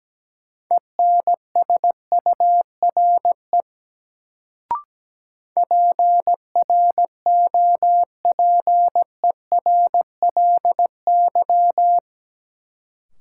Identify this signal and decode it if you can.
Morse code, heard as a keyed tone, about 17 wpm, 700 Hz. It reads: ENSURE PROPERLY
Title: Sentences from Top 1100 Words - New Word Drill - 17wpm